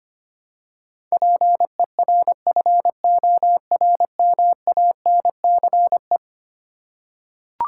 Morse code, heard as a keyed tone, about 25 wpm, 700 Hz. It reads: PERFORMANCE
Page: morse-code-ninja-practice-25wpm